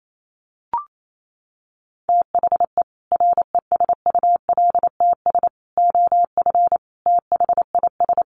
Morse code, heard as a keyed tone, about 28 wpm, 700 Hz. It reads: THE RESULTS OF THIS
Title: Sets of 5 Words 28wpm